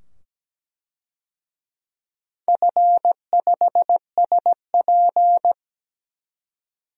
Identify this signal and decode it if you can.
Morse code, heard as a keyed tone, about 17 wpm, 700 Hz. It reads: F5SP
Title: DXCC-100 Call Signs - 4 Characters - 17wpm